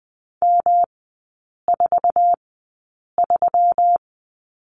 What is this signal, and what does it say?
Morse code, heard as a keyed tone, about 20 wpm, spaced as for 8 wpm, 690 Hz. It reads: M43
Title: 191 Lesson 31e - Letter v - Three Characters 20wpm